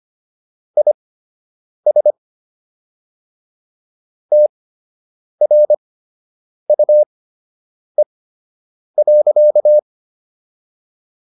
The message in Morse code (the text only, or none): IS TRUE.